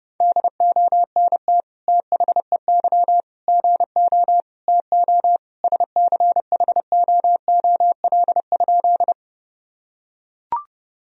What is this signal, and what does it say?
Morse code, heard as a keyed tone, about 30 wpm, 700 Hz. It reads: DONT THEY GO TO SCHOOL?
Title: Sentences from Top 1000 Words - Review Part 1 - 30wpm